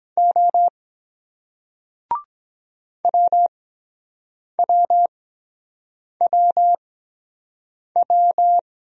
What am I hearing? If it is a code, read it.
Morse code, heard as a keyed tone, about 23 wpm, spaced as for 7 wpm, 700 Hz. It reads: O WWWW